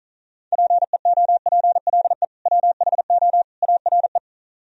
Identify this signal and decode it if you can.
Morse code, heard as a keyed tone, about 46 wpm, 700 Hz. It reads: PEOPLE WHO ARE